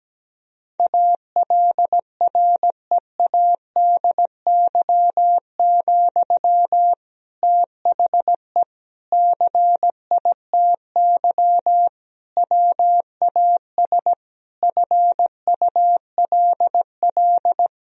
Morse code, heard as a keyed tone, about 17 wpm, 700 Hz. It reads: ALREADY, THE CITY WAS FULL
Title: Sentences from Top 1000 Words - Review Part 4 - 17wpm